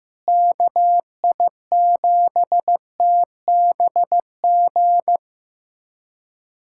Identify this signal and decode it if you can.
Morse code, heard as a keyed tone, about 15 wpm, 700 Hz. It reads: KI7TBG
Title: USA Callsign DE Callsign - 6 Characters - 15wpm